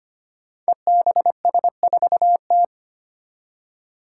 Morse code, heard as a keyed tone, about 25 wpm, 700 Hz. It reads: EBS4T
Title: Five Challenging Letter-Number Combinations 25wpm